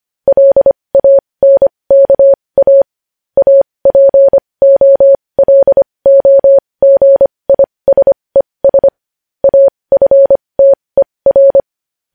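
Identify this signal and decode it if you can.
Morse code, heard as a keyed tone, about 25 wpm, spaced as for 20 wpm, 550 Hz. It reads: LANKA APOLOGISES AFTER
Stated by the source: Morse code news headlines sent at 20 wpm.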